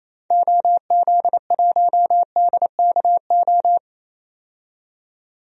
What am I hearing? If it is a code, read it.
Morse code, heard as a keyed tone, about 28 wpm, 700 Hz. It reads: OZ1DKO
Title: DXCC-100 Call Signs - 6 Characters - 28wpm